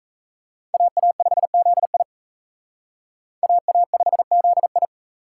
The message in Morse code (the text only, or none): AA5ZI AA5ZI